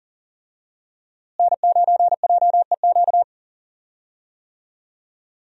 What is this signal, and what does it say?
Morse code, heard as a keyed tone, about 40 wpm, 700 Hz. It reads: N9JEQ
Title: USA Call Signs - 5 Characters - Repeated 3x - 40wpm